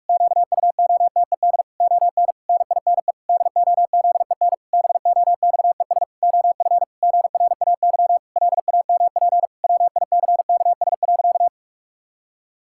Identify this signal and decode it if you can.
Morse code, heard as a keyed tone, about 41 wpm, 700 Hz. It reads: QUOTED ON NINE DOZEN BOXES OF GRAY LAMP WICKS.
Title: Pangram Challenge v2 - 45wpm